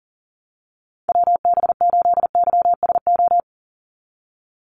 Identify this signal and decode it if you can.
Morse code, heard as a keyed tone, about 40 wpm, 690 Hz. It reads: WB8YSO
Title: USA Call Signs 40wpm